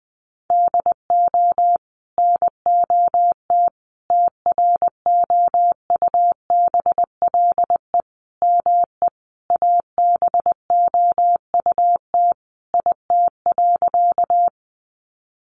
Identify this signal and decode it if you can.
Morse code, heard as a keyed tone, about 19 wpm, 700 Hz. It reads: DO NOT TROUBLE ME ABOUT IT.
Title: Sentences from Top 1200 Words 20wpm